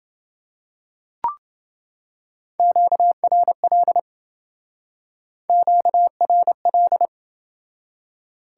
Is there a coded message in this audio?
Yes: Morse code from 2.6 to 7.1 s, heard as a keyed tone, about 27 wpm, 700 Hz. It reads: QRL QRL